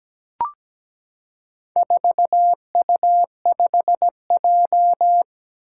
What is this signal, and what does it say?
Morse code, heard as a keyed tone, about 17 wpm, 700 Hz. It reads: 4U5J
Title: Four Challenging Letter-Number Combinations 17wpm